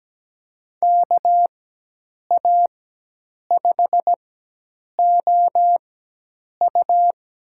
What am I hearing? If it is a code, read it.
Morse code, heard as a keyed tone, about 17 wpm, spaced as for 8 wpm, 700 Hz. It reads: KA5OU
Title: USA Call Signs - 4x Character Spacing - 17wpm